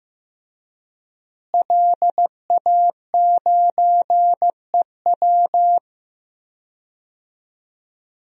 Morse code, heard as a keyed tone, about 15 wpm, 700 Hz. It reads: LA9EW
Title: DXCC-100 Call Signs - 5 Characters - Repeated 2x - 15wpm